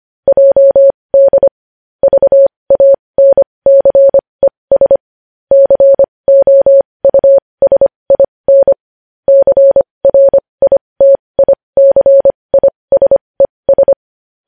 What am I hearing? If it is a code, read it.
Morse code, heard as a keyed tone, about 25 wpm, spaced as for 20 wpm, 550 Hz. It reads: JD VANCES COUSIN CRITICISES
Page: News Headlines in Morse Code at 20 WPM